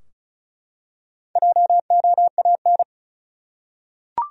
Morse code, heard as a keyed tone, about 34 wpm, 700 Hz. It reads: JOAN
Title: USA Common Names - 35wpm